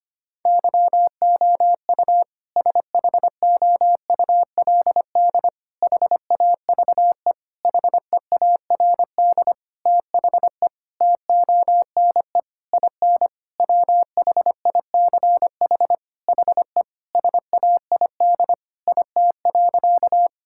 Morse code, heard as a keyed tone, about 25 wpm, 700 Hz. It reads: YOU SHOULD HAVE HEARD THE TONE IN WHICH HE SAID IT.
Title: Sentences from Top 1600 Words 25wpm